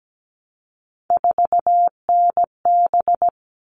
Morse code, heard as a keyed tone, about 17 wpm, 700 Hz. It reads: 4NB
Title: All Cut Numbers - 3-Digit 17wpm